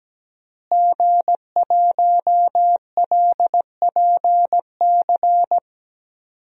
Morse code, heard as a keyed tone, about 17 wpm, 700 Hz. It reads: G1LPC
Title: DXCC-100 Call Signs - 5 Characters - Repeated 2x - 17wpm